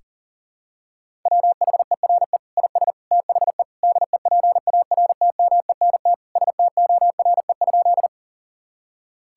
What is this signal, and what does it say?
Morse code, heard as a keyed tone, about 40 wpm, 700 Hz. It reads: WHERE IS THE DEPARTMENT STORE?
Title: Sentences from Top 1000 Words - Review Part 5 - 40wpm